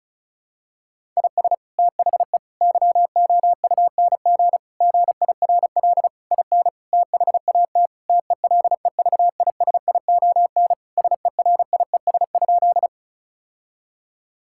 Morse code, heard as a keyed tone, about 34 wpm, 700 Hz. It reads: IS THE YOUNG GIRL IN THAT TELEVISION SERIES?
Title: Sentences from Top 1000 Words - Review Part 5 - 35wpm